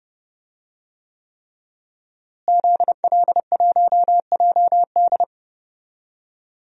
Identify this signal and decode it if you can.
Morse code, heard as a keyed tone, about 30 wpm, 700 Hz. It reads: ZL1JD
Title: DXCC-100 Call Signs - 5 Characters - Repeated 3x - 30wpm